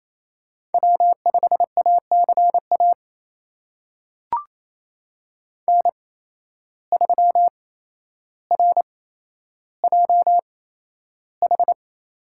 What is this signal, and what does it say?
Morse code, heard as a keyed tone, about 28 wpm, spaced as for 12 wpm, 700 Hz. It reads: W5ACA N3RJH
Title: USA Call Signs - 8x Character Spacing - 28wpm